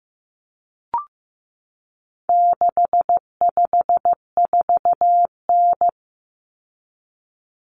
Morse code, heard as a keyed tone, about 15 wpm, 700 Hz. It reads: 654N
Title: Common Cut Numbers - 4-Digit 15wpm